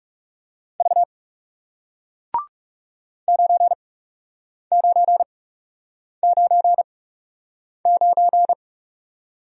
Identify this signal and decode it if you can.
Morse code, heard as a keyed tone, about 37 wpm, spaced as for 8 wpm, 700 Hz. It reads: V 9999
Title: Speed-Racing Single Letter-Number 30wpm